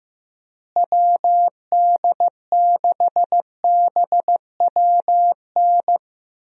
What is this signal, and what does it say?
Morse code, heard as a keyed tone, about 15 wpm, 700 Hz. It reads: WD6BWN
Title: USA Call Signs - 6 Characters - Repeated 2x - 15wpm